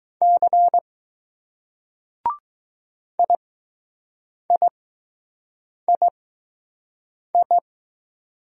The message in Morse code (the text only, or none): C IIII